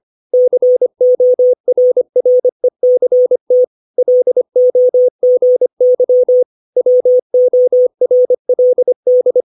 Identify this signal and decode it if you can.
Morse code, heard as a keyed tone, about 25 wpm, 500 Hz. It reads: CORRECT LOGY WORLD
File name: 632 words / 632 Words 25WPM